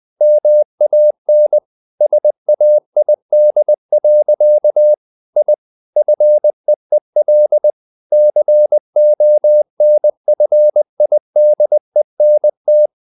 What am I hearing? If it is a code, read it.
Morse code, heard as a keyed tone, about 20 wpm, 600 Hz. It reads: MAN SAID. I FEEL CONFIDENT
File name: the old man and the sea / the old man and the sea part 1 20wpm